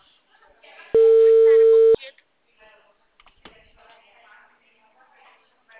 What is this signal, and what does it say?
Ringback tone (China, 450 Hz), with speech.
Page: Cordeiropolis 2